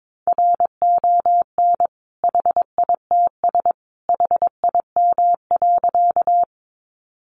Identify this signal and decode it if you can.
Morse code, heard as a keyed tone, about 22 wpm, 700 Hz. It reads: RON HITS HIM.
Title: Additional Easy Phrases 22wpm